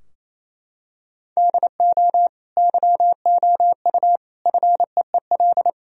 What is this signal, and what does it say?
Morse code, heard as a keyed tone, about 28 wpm, 700 Hz. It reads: DO YOU FEEL
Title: Sentences from Top 1000 Words - Review Part 3 - 28wpm